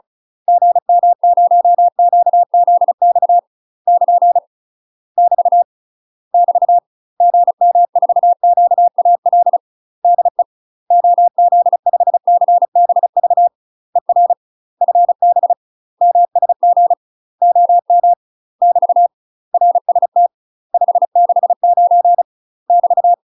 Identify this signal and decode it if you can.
Morse code, heard as a keyed tone, about 34 wpm, 700 Hz. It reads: GM0QZX <KN> <BT> <BT> GM4QAL DE OZ5CBV ER FB MSG OM <BT> RST 569 <BT>
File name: UK MASTER QSOs TRAINING / 35WPM / UK MASTER QSOs 0034